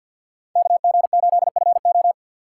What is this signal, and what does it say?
Morse code, heard as a keyed tone, about 53 wpm, 700 Hz. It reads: KG8FO
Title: USA Call Signs - 5 Characters - 4x Character Spacing - 50wpm